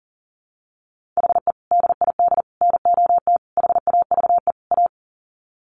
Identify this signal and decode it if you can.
Morse code, heard as a keyed tone, about 40 wpm, 700 Hz. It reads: HE DID NOT HAVE A
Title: Sets of 5 Words Encore 40wpm